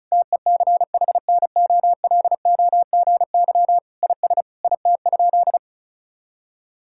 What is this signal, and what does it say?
Morse code, heard as a keyed tone, about 36 wpm, 700 Hz. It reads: TECHNOLOGY IS IT?
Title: Sentences from Top 1000 Words - Review Part 5 - 35wpm